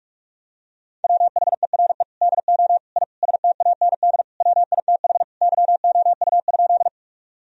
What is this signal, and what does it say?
Morse code, heard as a keyed tone, about 41 wpm, 700 Hz. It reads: WHERE DO I STAND WITH YOU?
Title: Sentences from Top 1000 Words - Review Part 1 - 45wpm